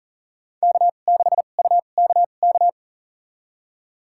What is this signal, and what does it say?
Morse code, heard as a keyed tone, about 40 wpm, spaced as for 29 wpm, 700 Hz. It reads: K6UKK
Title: USA Call Signs - 2x Character Spacing - 40wpm